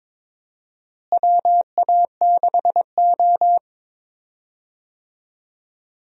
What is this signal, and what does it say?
Morse code, heard as a keyed tone, about 22 wpm, 700 Hz. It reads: WA6O
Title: USA Call Signs - 4 Characters - Repeated 2x - 22wpm